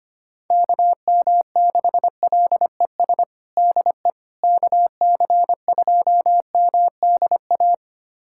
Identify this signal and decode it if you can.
Morse code, heard as a keyed tone, about 25 wpm, 700 Hz. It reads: KM6LES DE KC2MDA
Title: USA Callsign DE Callsign - 6 Characters - 25wpm